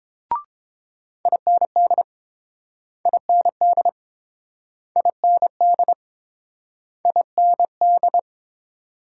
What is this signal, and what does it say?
Morse code, heard as a keyed tone, about 30 wpm, 700 Hz. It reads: IND IND IND IND